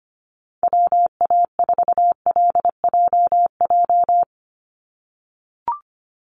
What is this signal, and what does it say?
Morse code, heard as a keyed tone, about 24 wpm, 700 Hz. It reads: WA4LJJ
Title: Call Signs - 25wpm